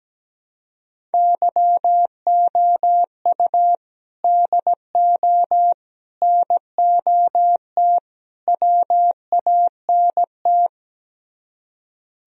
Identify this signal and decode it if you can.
Morse code, heard as a keyed tone, about 17 wpm, 700 Hz. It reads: YOU DO NOT WANT
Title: Speed-Racing - Sets of 4 Words 17wpm